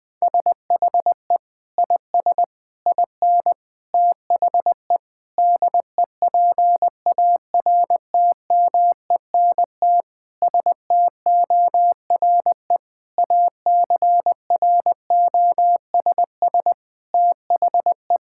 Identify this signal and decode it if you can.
Morse code, heard as a keyed tone, about 20 wpm, 700 Hz. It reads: SHE IS IN THE DEPARTMENT STORE ACROSS THE
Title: Sentences from Top 1000 Words - Review Part 5 - 20wpm